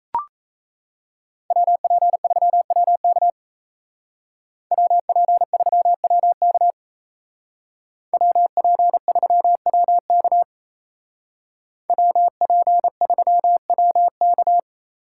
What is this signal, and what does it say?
Morse code, heard as a keyed tone, about 35 wpm, 700 Hz. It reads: WP3WK WP3WK WP3WK WP3WK